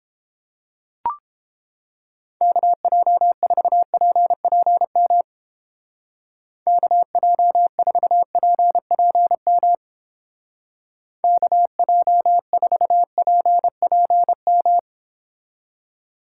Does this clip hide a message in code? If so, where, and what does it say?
Yes: Morse code from 2.4 to 14.8 s, heard as a keyed tone, about 31 wpm, 700 Hz. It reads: KJ4PPM KJ4PPM KJ4PPM